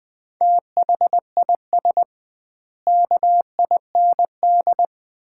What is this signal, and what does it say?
Morse code, heard as a keyed tone, about 21 wpm, 700 Hz. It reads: THIS KIND
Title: 2x Word-Spacing - Sets of 3 Words - 20wpm